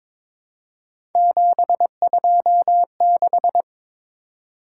Morse code, heard as a keyed tone, about 22 wpm, 700 Hz. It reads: 726
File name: practice / 22wpm / Three Letter-Number Combinations - 22wpm